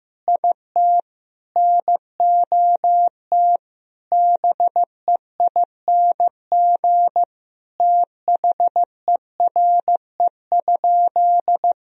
Morse code, heard as a keyed tone, about 15 wpm, 700 Hz. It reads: IT NOT BEING THERE?
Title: Sentences from Top 1000 Words - Review Part 3 - 15wpm